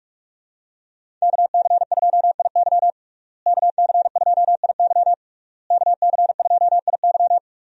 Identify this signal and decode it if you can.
Morse code, heard as a keyed tone, about 41 wpm, 700 Hz. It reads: KC2IY KC2IY KC2IY
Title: USA Call Signs - 5 Characters - Repeated 3x - 45wpm